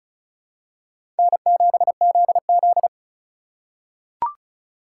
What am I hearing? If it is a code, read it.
Morse code, heard as a keyed tone, about 35 wpm, 700 Hz. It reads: N7ZZ
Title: USA Call Signs - 4 Characters - 4x Character Spacing - 35wpm